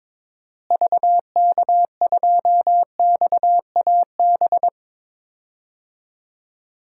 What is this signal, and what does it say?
Morse code, heard as a keyed tone, about 22 wpm, 700 Hz. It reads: VK2XAB
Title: DXCC-100 Call Signs - 6 Characters - Repeated 2x - 22wpm